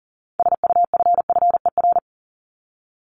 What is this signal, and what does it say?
Morse code, heard as a keyed tone, about 37 wpm, 700 Hz. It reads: SUFFER
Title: ING Suffix - Top 100 Words with Base Words 40wpm